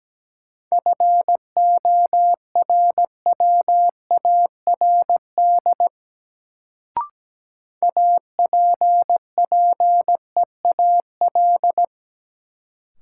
Morse code, heard as a keyed tone, about 17 wpm, 700 Hz. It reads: FORWARD APPEAL
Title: Sentences from Top 1300 Words - New Word Drill Alt - 17wpm